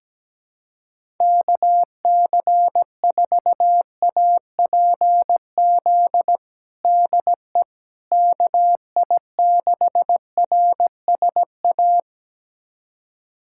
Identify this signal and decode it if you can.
Morse code, heard as a keyed tone, about 17 wpm, 700 Hz. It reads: KC4APZ DE KI6RSA
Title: USA Callsign DE Callsign - 6 Characters - 17wpm